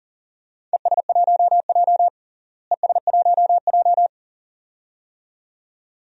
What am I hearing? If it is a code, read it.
Morse code, heard as a keyed tone, about 40 wpm, 700 Hz. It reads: ES1J ES1J
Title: DXCC-100 Call Signs - 4 Characters - Repeated 2x - 40wpm